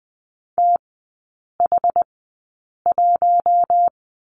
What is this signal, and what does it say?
Morse code, heard as a keyed tone, about 20 wpm, spaced as for 8 wpm, 700 Hz. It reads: TH1
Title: 081 Lesson 14e - Number 2 - Three Characters 20wpm